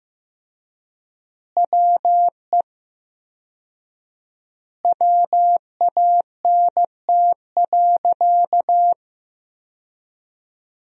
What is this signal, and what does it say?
Morse code, heard as a keyed tone, about 15 wpm, 700 Hz. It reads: WE WANT.
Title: 4x Word-Spacing - Sentences from Top 200 Words - 15wpm